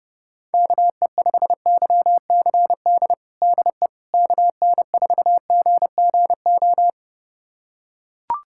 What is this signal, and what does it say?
Morse code, heard as a keyed tone, about 30 wpm, 700 Hz. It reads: KE5YCD DE KN4GGO
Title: USA Callsign DE Callsign - 6 Characters - 30wpm